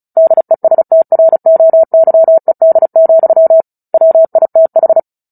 Morse code, heard as a keyed tone, about 34 wpm, 650 Hz. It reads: DESTROYED, WITH
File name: Book Two / 09-Chapter 9 - 35 WPM